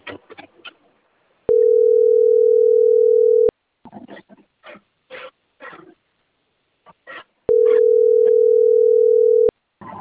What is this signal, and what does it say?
Ringback tone (North America), 440 + 480 Hz.